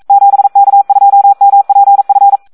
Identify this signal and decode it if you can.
Morse code, heard as a keyed tone, about 42 wpm, 800 Hz. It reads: 7K2MPF
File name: Audio practica CW / SD 1 42